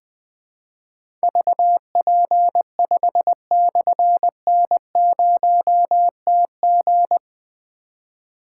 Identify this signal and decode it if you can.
Morse code, heard as a keyed tone, about 20 wpm, 700 Hz. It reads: VP5/N0TG
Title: Difficult Call Signs 20wpm